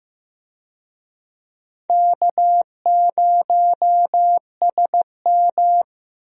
Morse code, heard as a keyed tone, about 15 wpm, 700 Hz. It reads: K0SM
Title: USA Call Signs - 4 Characters - Repeated 2x - 15wpm